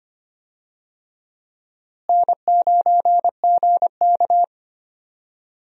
Morse code, heard as a keyed tone, about 25 wpm, 700 Hz. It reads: N9GK
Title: USA Call Signs - 4 Characters - Repeated 3x - 25wpm